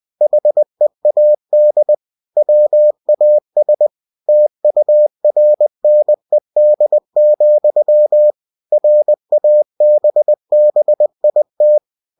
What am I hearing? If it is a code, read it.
Morse code, heard as a keyed tone, about 20 wpm, 600 Hz. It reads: HEAD WAS TURNED, RABBIT
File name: cw-books / winnie-the-pooh / winnie-20wpm-page-0069